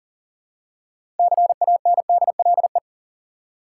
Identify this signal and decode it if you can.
Morse code, heard as a keyed tone, about 40 wpm, 700 Hz. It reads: CANDLE
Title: Six Letter Words - 40wpm